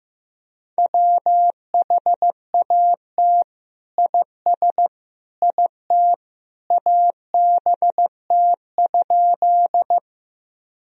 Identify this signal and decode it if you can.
Morse code, heard as a keyed tone, about 15 wpm, 700 Hz. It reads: WHAT IS IT ABT?
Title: Sentences from Top 100 Words and Abbreviations 15wpm